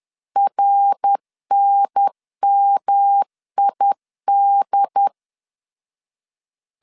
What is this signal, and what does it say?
Morse code, heard as a keyed tone, about 10 wpm, 800 Hz. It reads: RNMID
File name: CW / DET / 1 3 1500C S C 10wpm 3335